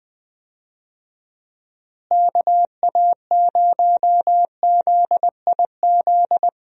Morse code, heard as a keyed tone, about 20 wpm, 700 Hz. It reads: KA0ZIZ
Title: USA Call Signs - 6 Characters - Repeated 3x - 20wpm